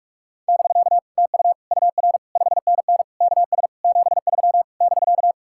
Morse code, heard as a keyed tone, about 63 wpm, spaced as for 54 wpm, 700 Hz. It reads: <BK> TU UR 5NN KS 73 <BK>
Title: Speed-Racing Common QSO Phrases Encore - 45wpm